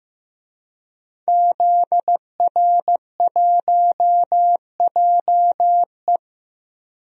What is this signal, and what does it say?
Morse code, heard as a keyed tone, about 15 wpm, 700 Hz. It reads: ZR1JE
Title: DXCC-100 Call Signs - 5 Characters - Repeated 3x - 15wpm